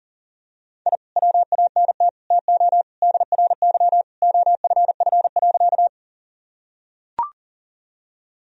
Morse code, heard as a keyed tone, about 40 wpm, 700 Hz. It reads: I WANT TO DRY OFF.
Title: Sentences from Top 1000 Words - Review Part 3 - 40wpm